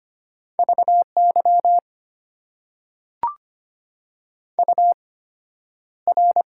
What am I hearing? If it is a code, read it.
Morse code, heard as a keyed tone, about 25 wpm, spaced as for 11 wpm, 690 Hz. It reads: VY UR
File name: Top 90 QSO Elements - 8x Character Spacing - 25wpm